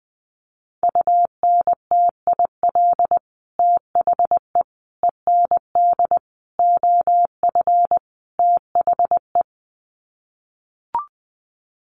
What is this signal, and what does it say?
Morse code, heard as a keyed tone, about 20 wpm, 700 Hz. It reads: UNTIL THE END OF THE
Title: Sets of 5 Words Encore 20wpm